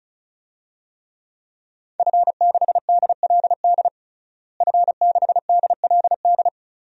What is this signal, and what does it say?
Morse code, heard as a keyed tone, about 36 wpm, 700 Hz. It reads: F6DLD F6DLD
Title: DXCC-100 Call Signs - 5 Characters - Repeated 3x - 35wpm